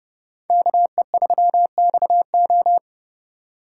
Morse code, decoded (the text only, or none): KE3XO